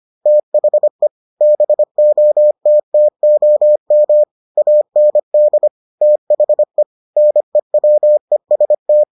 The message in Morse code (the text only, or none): THE BOTTOM AND THE NEWEST